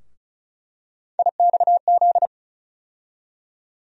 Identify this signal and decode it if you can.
Morse code, heard as a keyed tone, about 35 wpm, 700 Hz. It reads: IXZ
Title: USA Call Sign Postfixes - 3 Letters 35wpm